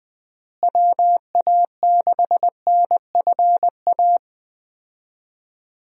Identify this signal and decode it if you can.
Morse code, heard as a keyed tone, about 20 wpm, 700 Hz. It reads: WA6NFA
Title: USA Call Signs - 6 Characters - Repeated 2x - 20wpm